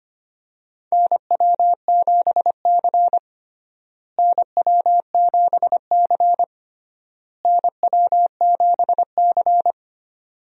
Morse code, heard as a keyed tone, about 25 wpm, 700 Hz. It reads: NW7C NW7C NW7C
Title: USA Call Signs - 4 Characters - Repeated 3x - 25wpm